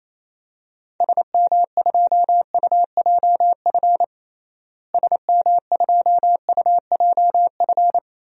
Morse code, heard as a keyed tone, about 28 wpm, 700 Hz. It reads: SM2UJF SM2UJF
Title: DXCC-100 Call Signs - 6 Characters - Repeated 2x - 28wpm